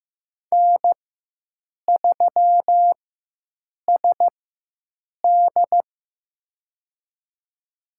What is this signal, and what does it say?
Morse code, heard as a keyed tone, about 15 wpm, spaced as for 7 wpm, 700 Hz. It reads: N3SD
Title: USA Call Signs - 4 Characters - 4x Character Spacing - 15wpm